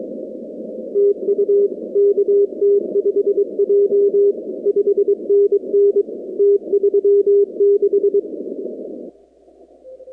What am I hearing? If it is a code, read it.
Morse code, heard as a keyed tone, about 23 wpm, 400 Hz. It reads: TU KT5J 5NN T36
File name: contests / 2019-cw-wpx-kt5j-k5pi- -2330 0-45